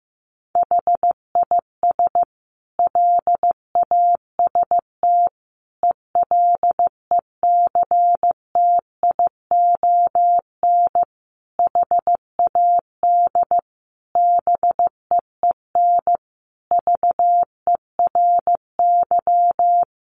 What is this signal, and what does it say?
Morse code, heard as a keyed tone, about 14 wpm, 700 Hz. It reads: HIS LAST ELECTION HAD BEEN VERY
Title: Sentences from Top 1500 Words 15wpm